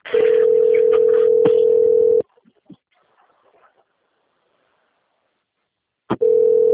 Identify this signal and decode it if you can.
Ringback tone (North America), 440 + 480 Hz.
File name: out- - - - - .50430